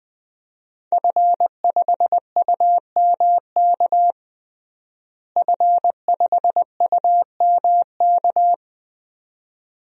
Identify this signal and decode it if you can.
Morse code, heard as a keyed tone, about 21 wpm, 700 Hz. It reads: F5UMK F5UMK
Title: DXCC-100 Call Signs - 5 Characters - Repeated 2x - 20wpm